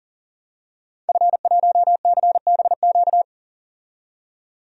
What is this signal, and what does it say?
Morse code, heard as a keyed tone, about 40 wpm, 700 Hz. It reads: F1CBQ
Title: DXCC-100 Call Signs - 5 Characters - 40wpm